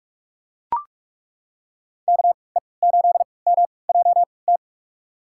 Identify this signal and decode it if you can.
Morse code, heard as a keyed tone, about 41 wpm, spaced as for 25 wpm, 700 Hz. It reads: KE8MJT
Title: USA Call Signs - 6 Characters - 3x Character Spacing - 45wpm